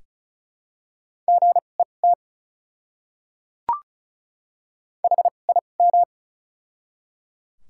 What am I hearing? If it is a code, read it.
Morse code, heard as a keyed tone, about 36 wpm, spaced as for 25 wpm, 700 Hz. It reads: GET HIM
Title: 2x Character Spacing - Top 200 Words - 35wpm